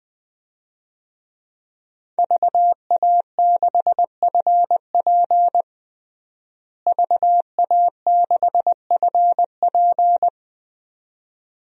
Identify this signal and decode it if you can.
Morse code, heard as a keyed tone, about 20 wpm, 700 Hz. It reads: VA6FP VA6FP
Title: DXCC-100 Call Signs - 5 Characters - Repeated 2x - 20wpm